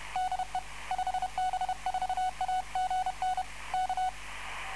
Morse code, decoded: DE 5B4AGN K